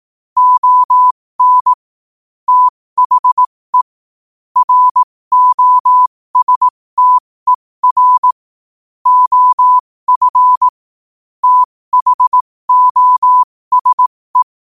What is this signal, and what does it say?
Morse code, heard as a keyed tone, about 18 wpm, spaced as for 15 wpm, 1000 Hz. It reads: ON THE ROSTER OF THOSE
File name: qotdfiles / 15wpm